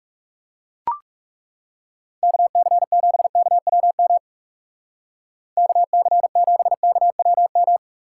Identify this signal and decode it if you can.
Morse code, heard as a keyed tone, about 36 wpm, 700 Hz. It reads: KC7KWM KC7KWM
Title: Speed Racing - USA 2x3 Call Signs 30wpm